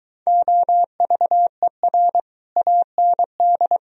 Morse code, heard as a keyed tone, about 23 wpm, 700 Hz. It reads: OVER AND